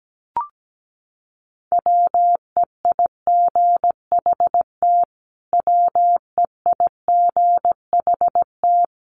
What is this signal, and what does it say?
Morse code, heard as a keyed tone, about 17 wpm, 700 Hz. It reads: WEIGHT WEIGHT
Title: Top 500 Words Repeated 17wpm